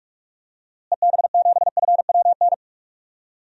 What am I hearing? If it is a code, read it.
Morse code, heard as a keyed tone, about 44 wpm, 700 Hz. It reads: EB7FWN
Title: DXCC-100 Call Signs - 6 Characters - Repeated 2x - 45wpm